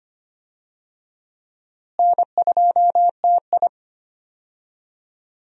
Morse code, heard as a keyed tone, about 25 wpm, 700 Hz. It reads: N2TI
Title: USA Call Signs - 4 Characters - Repeated 2x - 25wpm